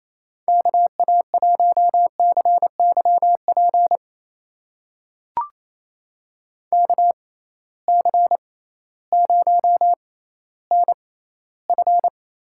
Morse code, heard as a keyed tone, about 28 wpm, spaced as for 15 wpm, 700 Hz. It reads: KA1CYP KC0NF
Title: USA Call Signs - 6 Characters - 6x Character Spacing - 28wpm